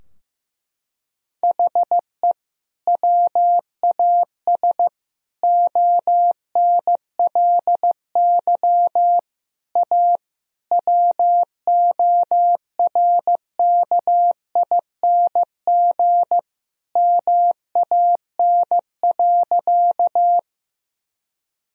Morse code, heard as a keyed tone, about 15 wpm, 700 Hz. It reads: HE WAS ONLY A WORKING MAN.
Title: Sentences from Top 1000 Words - Review Part 4 - 15wpm